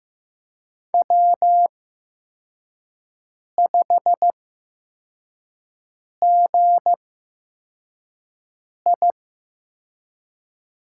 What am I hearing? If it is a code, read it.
Morse code, heard as a keyed tone, about 15 wpm, spaced as for 4 wpm, 700 Hz. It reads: W5GI
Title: USA Call Signs - 4 Characters - 8x Character Spacing - 15wpm